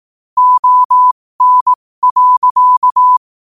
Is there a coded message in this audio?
Yes: Morse code from 0.4 to 3.2 s, heard as a keyed tone, about 18 wpm, spaced as for 15 wpm, 1000 Hz. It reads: ON.